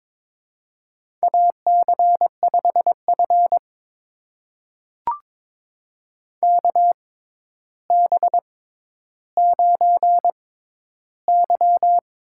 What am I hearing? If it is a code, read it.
Morse code, heard as a keyed tone, about 22 wpm, spaced as for 11 wpm, 700 Hz. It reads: AC5F KB9Y
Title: USA Call Signs - 4 Characters - 6x Character Spacing - 22wpm